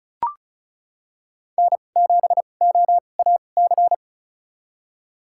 Morse code, heard as a keyed tone, about 36 wpm, spaced as for 25 wpm, 700 Hz. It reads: N7OAC
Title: USA Call Signs - 5 Characters - 2x Character Spacing - 35wpm